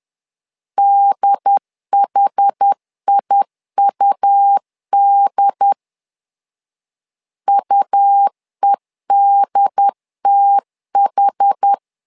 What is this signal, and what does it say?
Morse code, heard as a keyed tone, about 10 wpm, 800 Hz. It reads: DHIUD UEDTH
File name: LEZIONI / 2 UHDN / 2 UHD 10wpm 1826